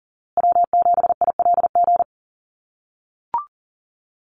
Morse code, heard as a keyed tone, about 40 wpm, 700 Hz. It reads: W7ILG
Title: USA Call Signs Repeated 3x 40wpm